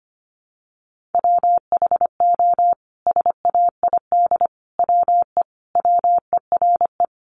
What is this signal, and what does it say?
Morse code, heard as a keyed tone, about 25 wpm, 700 Hz. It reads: WHO SAID WE WERE
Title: Sentences from Top 700 Words 25wpm